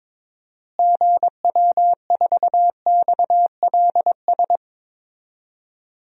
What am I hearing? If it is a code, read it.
Morse code, heard as a keyed tone, about 21 wpm, 700 Hz. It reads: GW4XLS
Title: DXCC-100 Call Signs - 6 Characters - Repeated 3x - 22wpm